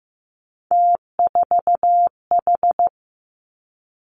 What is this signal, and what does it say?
Morse code, heard as a keyed tone, about 15 wpm, 700 Hz. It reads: T4H
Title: Three Challenging Letter-Number Combinations 15wpm